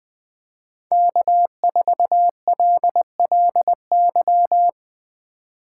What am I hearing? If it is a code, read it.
Morse code, heard as a keyed tone, about 21 wpm, 700 Hz. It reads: K4LLY
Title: USA Call Signs - 5 Characters - 4x Character Spacing - 20wpm